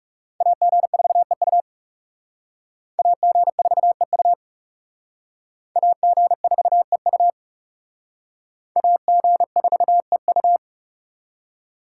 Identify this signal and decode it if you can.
Morse code, heard as a keyed tone, about 35 wpm, 700 Hz. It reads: AG4EU AG4EU AG4EU AG4EU
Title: Speed Racing - USA 2x2 Call Signs 30wpm